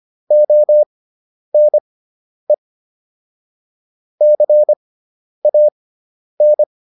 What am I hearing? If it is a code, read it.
Morse code, heard as a keyed tone, about 25 wpm, spaced as for 10 wpm, 600 Hz. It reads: ONE CAN